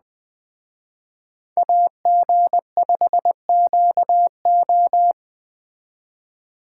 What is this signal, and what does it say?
Morse code, heard as a keyed tone, about 20 wpm, 700 Hz. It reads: AG5QO
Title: Speed Racing - USA Call Signs 17wpm